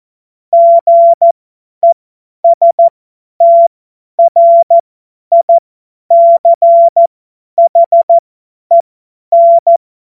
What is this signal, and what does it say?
Morse code, heard as a keyed tone, about 14 wpm, spaced as for 10 wpm, 680 Hz. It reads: GESTRICHEN
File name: Morsekurs / Witze / Witz 23 Januar 2024 50 F70